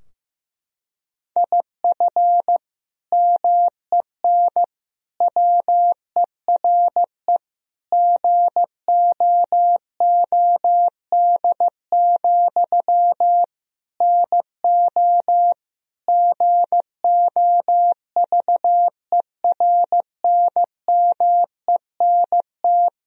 Morse code, heard as a keyed tone, about 15 wpm, 700 Hz. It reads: IF MEN WERE GOOD, NO GOVERNMENT
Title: Sentences from Top 1000 Words - Review Part 3 - 15wpm